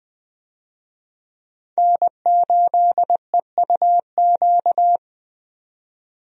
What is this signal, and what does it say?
Morse code, heard as a keyed tone, about 20 wpm, 700 Hz. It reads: N8EUQ
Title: USA Call Signs - 5 Characters - 20wpm